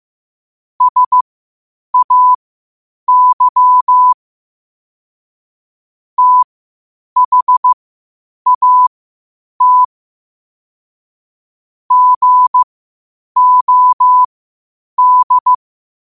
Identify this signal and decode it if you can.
Morse code, heard as a keyed tone, about 15 wpm, spaced as for 8 wpm, 1000 Hz. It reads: SAY THAT GOD
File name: qotdfiles / 8wpm